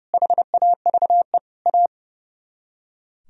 Morse code, heard as a keyed tone, about 37 wpm, 700 Hz. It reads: HAVE A
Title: Speed-Racing - Sets of 5 Words 30wpm